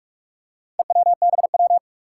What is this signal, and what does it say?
Morse code, heard as a keyed tone, about 44 wpm, 700 Hz. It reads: EWBW
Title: Four Challenging Letter-Number Combinations 45wpm